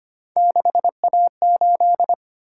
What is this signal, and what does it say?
Morse code, heard as a keyed tone, about 25 wpm, 700 Hz. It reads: 6A8
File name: Three Letter-Number Combinations - 25wpm